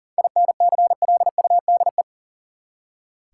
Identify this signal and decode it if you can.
Morse code, heard as a keyed tone, about 39 wpm, 700 Hz. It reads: INCLUDE
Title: Seven Letter Words - 40wpm